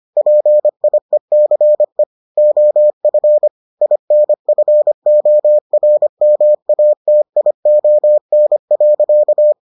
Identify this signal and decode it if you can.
Morse code, heard as a keyed tone, about 25 wpm, 600 Hz. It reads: PIECE OF INFORMATION.